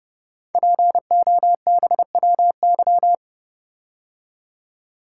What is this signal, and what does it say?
Morse code, heard as a keyed tone, about 30 wpm, 700 Hz. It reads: POBWY
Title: Code Groups - 5 Letters - 30wpm